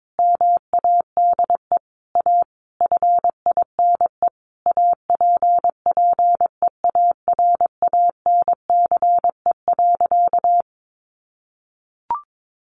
Morse code, heard as a keyed tone, about 22 wpm, 700 Hz. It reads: MADE A FINE APPEARANCE.
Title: Sentences from Top 1100 Words 22wpm